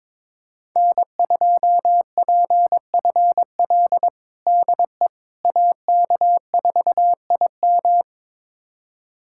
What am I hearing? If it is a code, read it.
Morse code, heard as a keyed tone, about 21 wpm, 700 Hz. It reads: N2PFL DE AK4IM
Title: USA Callsign DE Callsign - 5 Characters - 22wpm